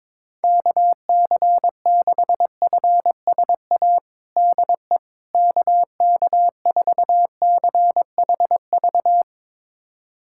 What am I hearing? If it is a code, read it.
Morse code, heard as a keyed tone, about 22 wpm, 700 Hz. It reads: KC6FSA DE KK4CHV
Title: USA Callsign DE Callsign 22wpm